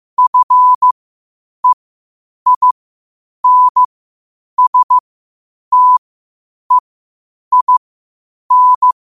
Morse code, heard as a keyed tone, about 15 wpm, spaced as for 8 wpm, 1000 Hz. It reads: FEINSTEIN
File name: qotdfiles / 8wpm